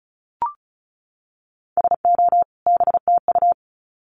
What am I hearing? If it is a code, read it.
Morse code, heard as a keyed tone, about 34 wpm, 700 Hz. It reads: SO BTU
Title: Common QSO Phrases 35wpm